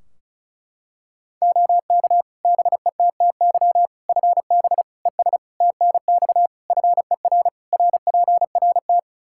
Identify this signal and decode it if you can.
Morse code, heard as a keyed tone, about 36 wpm, 700 Hz. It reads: OK BETTY FB ES TNX FER RPRT
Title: Practice QSO - Exchange 2 - Info - 35wpm